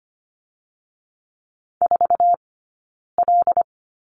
Morse code, heard as a keyed tone, about 25 wpm, 700 Hz. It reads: 4 L
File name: course / 25:10wpm / 092 Lesson 16d - Letter c - Two Characters 25wpm